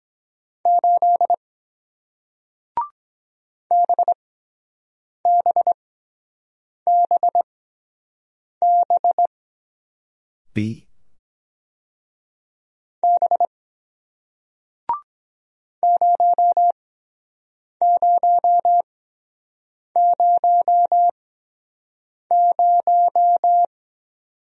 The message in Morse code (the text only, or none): 8 BBBB B 0000